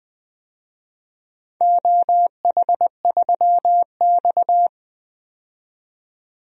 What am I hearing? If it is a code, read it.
Morse code, heard as a keyed tone, about 21 wpm, 700 Hz. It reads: OH3X
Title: DXCC-100 Call Signs - 4 Characters - 20wpm